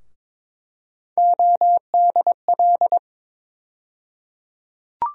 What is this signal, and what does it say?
Morse code, heard as a keyed tone, about 21 wpm, 700 Hz. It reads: ODL
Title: USA Call Sign Postfixes - 3 Letters 22wpm